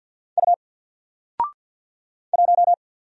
Morse code, heard as a keyed tone, about 48 wpm, 700 Hz. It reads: U 1
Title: Single Character (Letter:Number:Punctuation:Prosign) Alt 50wpm